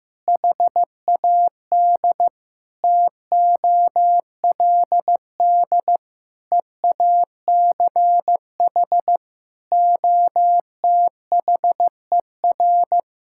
Morse code, heard as a keyed tone, about 16 wpm, 700 Hz. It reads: HAD TOLD EACH OTHER
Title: Sentences from Top 1600 Words 15wpm